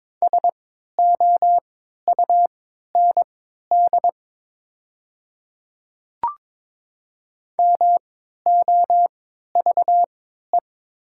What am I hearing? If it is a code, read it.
Morse code, heard as a keyed tone, about 22 wpm, spaced as for 12 wpm, 700 Hz. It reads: SOUND MOVE
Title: 3x Character Spacing - Top 500 Words - 22wpm